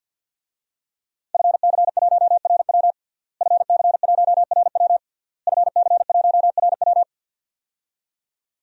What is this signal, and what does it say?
Morse code, heard as a keyed tone, about 51 wpm, 700 Hz. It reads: FC1RW FC1RW FC1RW
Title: DXCC-100 Call Signs - 5 Characters - Repeated 3x - 50wpm